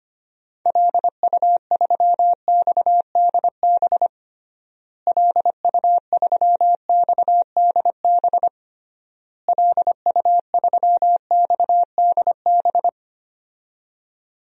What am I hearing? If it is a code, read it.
Morse code, heard as a keyed tone, about 25 wpm, 700 Hz. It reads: LU3XDB LU3XDB LU3XDB
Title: DXCC-100 Call Signs - 6 Characters - Repeated 3x - 25wpm